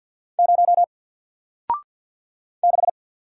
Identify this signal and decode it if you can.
Morse code, heard as a keyed tone, about 50 wpm, 700 Hz. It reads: 0 6
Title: Single Character (Letter:Number:Punctuation:Prosign) Alt 50wpm